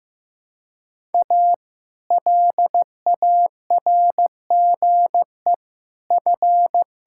Morse code, heard as a keyed tone, about 15 wpm, 700 Hz. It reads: A LARGE F
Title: Pangram Challenge v2 - 15wpm